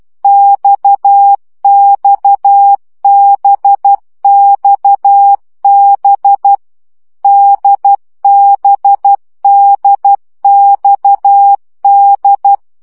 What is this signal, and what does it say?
Morse code, heard as a keyed tone, about 12 wpm, 800 Hz. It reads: XXBXB DBDXD
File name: CW gyakorlas d b x 10 baud